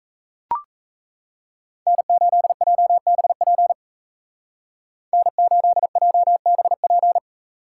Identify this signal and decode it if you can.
Morse code, heard as a keyed tone, about 35 wpm, 700 Hz. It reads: N8JBP N8JBP
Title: Speed Racing - USA 1x3 Call Signs 28wpm